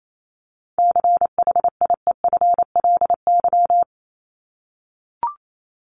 Morse code, heard as a keyed tone, about 28 wpm, 700 Hz. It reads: CHIEFLY
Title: LY Suffix - Top 100 Words with Base Words 28wpm